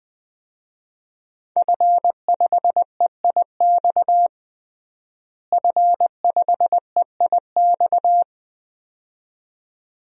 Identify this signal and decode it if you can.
Morse code, heard as a keyed tone, about 21 wpm, 700 Hz. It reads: F5EIX F5EIX
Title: DXCC-100 Call Signs - 5 Characters - Repeated 2x - 20wpm